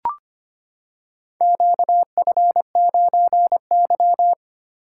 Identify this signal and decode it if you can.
Morse code, heard as a keyed tone, about 25 wpm, 700 Hz. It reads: QF9Y
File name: practice / 25wpm / Four Letter-Number Combinations - 25wpm